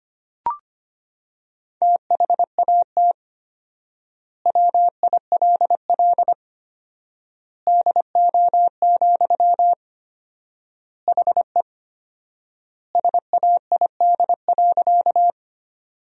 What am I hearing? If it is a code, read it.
Morse code, heard as a keyed tone, about 25 wpm, 700 Hz. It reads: THAT WILL DO, HE SAID.
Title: 4x Word-Spacing - Sentences from Top 100 Words - 25wpm